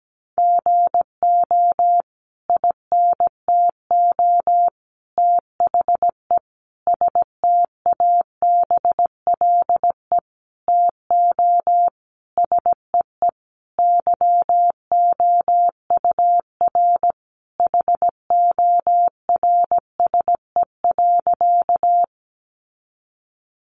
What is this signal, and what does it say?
Morse code, heard as a keyed tone, about 17 wpm, 700 Hz. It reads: GO INTO THE STABLE TO SEE YOUR HORSE.
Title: Sentences from Top 1500 Words 17wpm